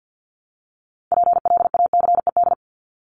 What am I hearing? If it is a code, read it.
Morse code, heard as a keyed tone, about 50 wpm, 700 Hz. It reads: PLACED
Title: ED Suffix - Top 100 Words with Base Words 50wpm